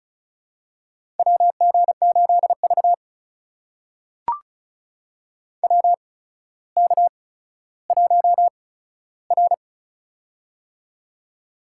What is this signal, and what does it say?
Morse code, heard as a keyed tone, about 36 wpm, spaced as for 15 wpm, 700 Hz. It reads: WG8V WK1R
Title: USA Call Signs - 4 Characters - 8x Character Spacing - 35wpm